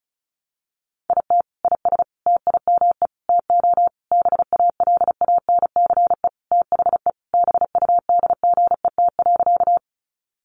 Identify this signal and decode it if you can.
Morse code, heard as a keyed tone, about 34 wpm, 700 Hz. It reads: IT IS TIME TO BALANCE THE BUDGET.
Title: Sentences from Top 1200 Words 35wpm